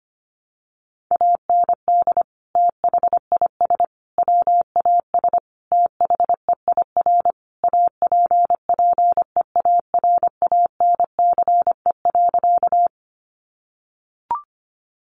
Morse code, heard as a keyed tone, about 25 wpm, 700 Hz. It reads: AND THIS WAS THEIR APPEARANCE.
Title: Sentences from Top 1100 Words 25wpm